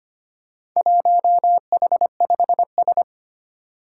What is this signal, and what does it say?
Morse code, heard as a keyed tone, about 25 wpm, 700 Hz. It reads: 1H5S
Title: Four Challenging Letter-Number Combinations 25wpm